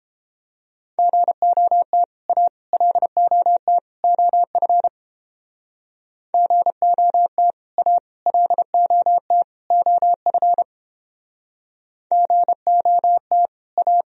Morse code, heard as a keyed tone, about 30 wpm, 700 Hz. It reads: GOT A LOT OF GOT A LOT OF GOT A